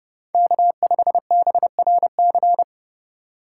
Morse code, heard as a keyed tone, about 31 wpm, 700 Hz. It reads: K5BRC